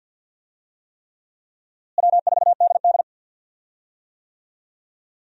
Morse code, heard as a keyed tone, about 48 wpm, 700 Hz. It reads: W4DD
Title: USA Call Signs - 4 Characters - Repeated 3x - 50wpm